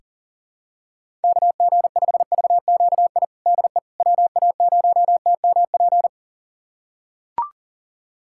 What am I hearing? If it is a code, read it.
Morse code, heard as a keyed tone, about 40 wpm, 700 Hz. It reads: KG5VQI DE WA0TMP
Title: USA Callsign DE Callsign 40wpm